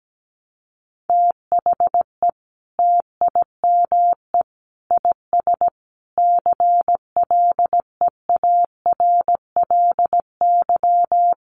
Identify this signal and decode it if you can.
Morse code, heard as a keyed tone, about 17 wpm, 700 Hz. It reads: THE TIME IS CLEARLY
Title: Sentences from Top 1300 Words 17wpm